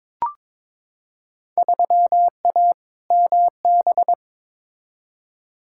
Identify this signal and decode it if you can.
Morse code, heard as a keyed tone, about 22 wpm, 700 Hz. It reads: 3A MB
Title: Field Day Exchanges Encore - 22wpm